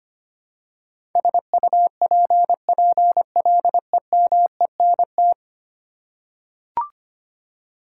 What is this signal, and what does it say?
Morse code, heard as a keyed tone, about 25 wpm, 700 Hz. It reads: SUPPLEMENT
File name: 10 Letter Words - First 3 Letters Given - 25wpm